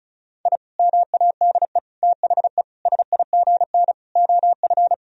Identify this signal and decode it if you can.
Morse code, heard as a keyed tone, about 36 wpm, 700 Hz. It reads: I MADE THE SIGN OF
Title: Sentences from Top 1000 Words - Review Part 1 - 35wpm